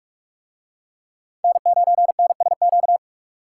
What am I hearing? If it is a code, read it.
Morse code, heard as a keyed tone, about 42 wpm, 700 Hz. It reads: N9NSQ